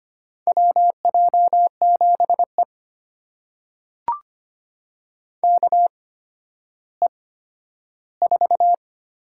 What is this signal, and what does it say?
Morse code, heard as a keyed tone, about 25 wpm, spaced as for 12 wpm, 700 Hz. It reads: WJ7E KE4
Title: USA Call Signs - 8x Character Spacing - 25wpm